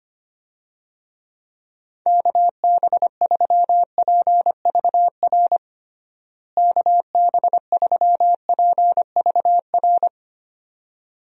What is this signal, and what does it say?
Morse code, heard as a keyed tone, about 25 wpm, 700 Hz. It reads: KB3PVR KB3PVR
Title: USA Call Signs - 6 Characters - Repeated 2x - 25wpm